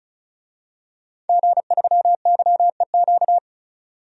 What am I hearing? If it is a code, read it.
Morse code, heard as a keyed tone, about 34 wpm, 700 Hz. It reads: G3YEQ
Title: DXCC-100 Call Signs - 5 Characters - 35wpm